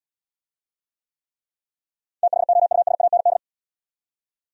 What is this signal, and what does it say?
Morse code, heard as a keyed tone, about 80 wpm, 700 Hz. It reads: THOUSAND